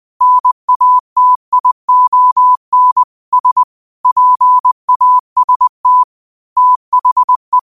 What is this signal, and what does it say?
Morse code, heard as a keyed tone, about 20 wpm, 1000 Hz. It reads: NATION S PAST THE
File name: qotdfiles / 20wpm